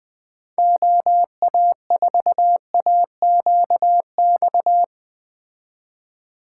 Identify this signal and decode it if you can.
Morse code, heard as a keyed tone, about 20 wpm, 700 Hz. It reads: OA4AQX
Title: Speed Racing - USA 2x3 Call Signs 17wpm